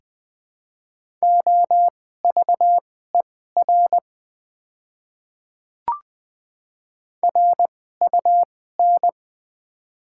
Morse code, heard as a keyed tone, about 20 wpm, spaced as for 15 wpm, 700 Hz. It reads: OVER RUN
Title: 2x Character Spacing - Top 300 Words - 20wpm